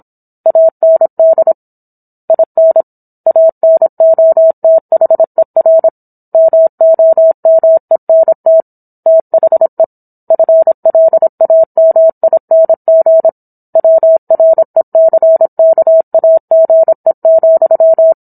Morse code, heard as a keyed tone, about 27 wpm, 650 Hz. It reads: AND IN ANOTHER MOMENT THE FLAMING WRECKAGE,